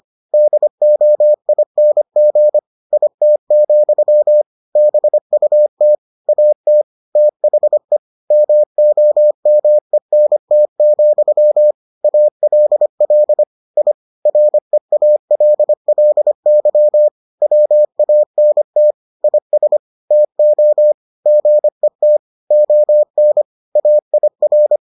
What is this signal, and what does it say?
Morse code, heard as a keyed tone, about 25 wpm, 600 Hz. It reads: DOING IT, BUT AT THE MOMENT, ALL I REALLY WANT IS TO GET ON AIR